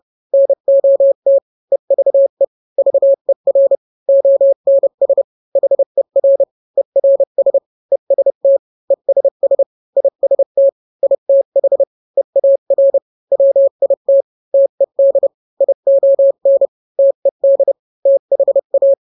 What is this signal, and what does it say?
Morse code, heard as a keyed tone, about 30 wpm, 550 Hz. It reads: NOT EVE VER ONS HER ERS EST ESS IST ITH EAR WIT TED ION TED THA
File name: n-grams / 3-grams 30 25 WPM